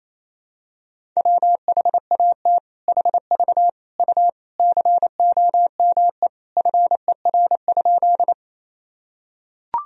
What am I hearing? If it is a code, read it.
Morse code, heard as a keyed tone, about 28 wpm, 700 Hz. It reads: WHAT HV U COME FER?
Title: Sentences from Top 100 Words and Abbreviations 28wpm